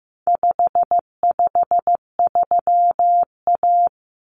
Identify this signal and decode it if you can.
Morse code, heard as a keyed tone, about 15 wpm, 700 Hz. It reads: 553A
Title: Common Cut Numbers - 4-Digit 15wpm